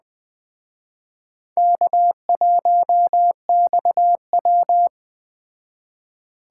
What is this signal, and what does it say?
Morse code, heard as a keyed tone, about 19 wpm, 700 Hz. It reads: K1XW